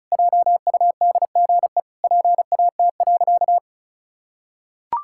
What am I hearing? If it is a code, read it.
Morse code, heard as a keyed tone, about 34 wpm, 700 Hz. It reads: JUDGE PAT.
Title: Pangram Challenge v2 - 35wpm